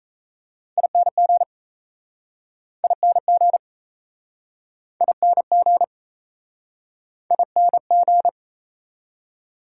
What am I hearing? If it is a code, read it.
Morse code, heard as a keyed tone, about 36 wpm, 700 Hz. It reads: ING ING ING ING